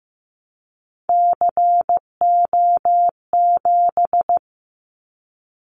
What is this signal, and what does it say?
Morse code, heard as a keyed tone, about 15 wpm, 700 Hz. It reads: CO7
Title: Callsign Prefixes for Top 100 DXCC - 15wpm